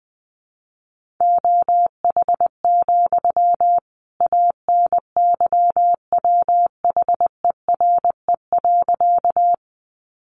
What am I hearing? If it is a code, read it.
Morse code, heard as a keyed tone, about 20 wpm, 700 Hz. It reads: OH, ANYWHERE.
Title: Sentences from Top 1400 Words 20wpm